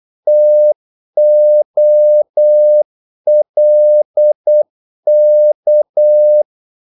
Morse code, heard as a keyed tone, about 8 wpm, 600 Hz. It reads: TOLK
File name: Korte ord DK 8wpm